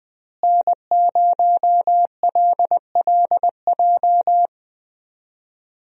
Morse code, heard as a keyed tone, about 20 wpm, 700 Hz. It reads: N0LLJ
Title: USA Call Signs - 5 Characters - Repeated 2x - 20wpm